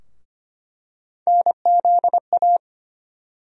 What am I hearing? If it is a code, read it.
Morse code, heard as a keyed tone, about 25 wpm, 700 Hz. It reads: NZA
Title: USA Call Sign Postfixes - 3 Letters 25wpm